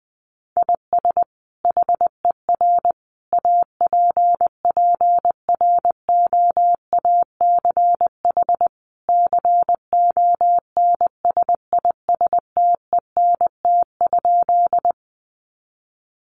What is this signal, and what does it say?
Morse code, heard as a keyed tone, about 19 wpm, 700 Hz. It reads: IS HER APPROACH CONSISTENT?
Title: Sentences from Top 1500 Words 20wpm